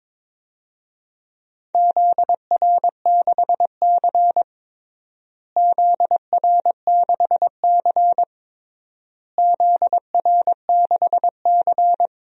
Morse code, heard as a keyed tone, about 21 wpm, 700 Hz. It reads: ZR6C ZR6C ZR6C
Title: DXCC-100 Call Signs - 4 Characters - Repeated 3x - 22wpm